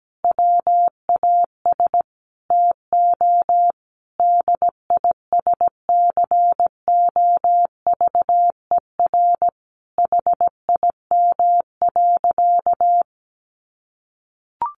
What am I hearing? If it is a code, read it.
Morse code, heard as a keyed tone, about 17 wpm, 700 Hz. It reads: WAS TO DISCOVER HIM.
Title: Sentences from Top 1100 Words 17wpm